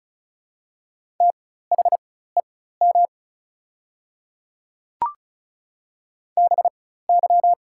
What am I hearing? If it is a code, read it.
Morse code, heard as a keyed tone, about 36 wpm, spaced as for 16 wpm, 700 Hz. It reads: THEM BY
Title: 4x Character Spacing - Top 100 Words - 35wpm